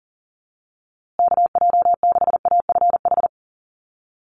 Morse code, heard as a keyed tone, about 40 wpm, 700 Hz. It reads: KJ6AFH
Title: USA Call Signs Repeated 3x 40wpm